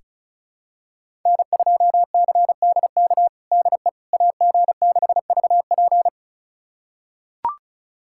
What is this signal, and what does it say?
Morse code, heard as a keyed tone, about 36 wpm, 700 Hz. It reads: N2CDK DE AG6VP
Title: USA Callsign DE Callsign - 5 Characters - 35wpm